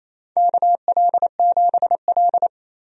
Morse code, heard as a keyed tone, about 29 wpm, 700 Hz. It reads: KL7L
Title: DXCC-100 Call Signs - 4 Characters - 28wpm